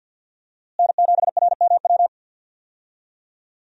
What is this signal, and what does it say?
Morse code, heard as a keyed tone, about 48 wpm, 700 Hz. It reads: N7RMW
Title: USA Call Signs - 5 Characters - Repeated 3x - 50wpm